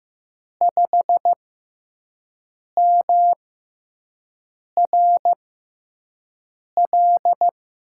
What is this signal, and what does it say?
Morse code, heard as a keyed tone, about 15 wpm, spaced as for 5 wpm, 700 Hz. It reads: 5MRL
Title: USA Call Signs - 5 Characters - 6x Character Spacing - 15wpm